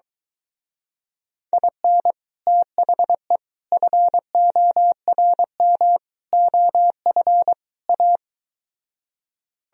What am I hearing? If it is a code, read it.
Morse code, heard as a keyed tone, about 23 wpm, 700 Hz. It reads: IN THE FORM OF A